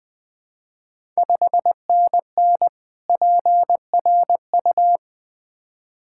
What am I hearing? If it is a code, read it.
Morse code, heard as a keyed tone, about 20 wpm, 690 Hz. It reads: 5NN PRU
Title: Ontario QSO Party Exchange 20wpm